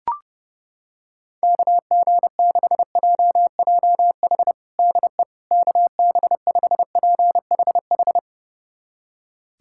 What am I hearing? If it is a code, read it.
Morse code, heard as a keyed tone, about 30 wpm, 700 Hz. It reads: KG6JJH DE KB5PHH
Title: USA Callsign DE Callsign 30wpm